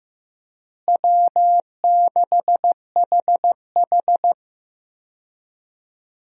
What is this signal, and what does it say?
Morse code, heard as a keyed tone, about 15 wpm, 700 Hz. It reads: W6HH
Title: USA Call Signs - 4 Characters - Repeated 2x - 15wpm